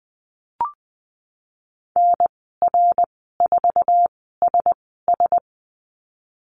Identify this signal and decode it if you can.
Morse code, heard as a keyed tone, about 20 wpm, spaced as for 15 wpm, 700 Hz. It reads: NR4SS
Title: 082 Lesson 14g - Number 2 - Call Signs 20wpm